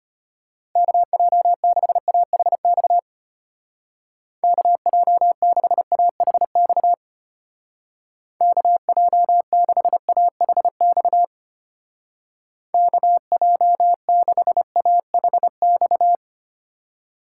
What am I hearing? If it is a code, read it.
Morse code, heard as a keyed tone, about 32 wpm, 700 Hz. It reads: KJ6AHX KJ6AHX KJ6AHX KJ6AHX